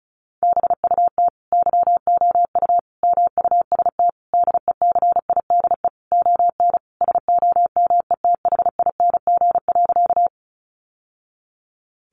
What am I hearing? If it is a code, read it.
Morse code, heard as a keyed tone, about 34 wpm, 700 Hz. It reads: BUT YOU MUST DECIDE ON SOMETHING.
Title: Sentences from Top 600 Words 35wpm